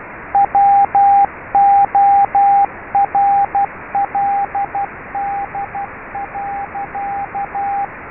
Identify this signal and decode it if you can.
Morse code, heard as a keyed tone, about 12 wpm, 800 Hz, with radio noise. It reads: WORLD.
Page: MORSE CODE TRAINER AND SIMULATOR